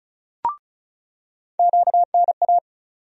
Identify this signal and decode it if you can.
Morse code, heard as a keyed tone, about 36 wpm, 700 Hz. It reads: QNA
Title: USA Call Sign Postfixes - 3 Letters 35wpm